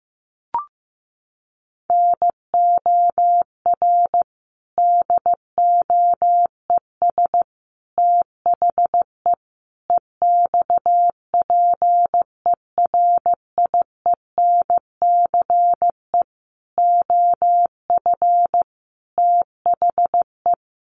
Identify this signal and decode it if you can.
Morse code, heard as a keyed tone, about 15 wpm, 700 Hz. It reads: NOR DOES THE EXPERIENCE OF THE
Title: Sentences from Top 1500 Words 15wpm